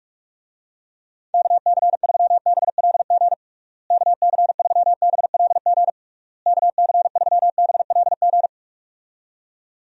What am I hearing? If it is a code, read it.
Morse code, heard as a keyed tone, about 46 wpm, 700 Hz. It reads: KC3BLG KC3BLG KC3BLG
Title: USA Call Signs - 6 Characters - Repeated 3x - 45wpm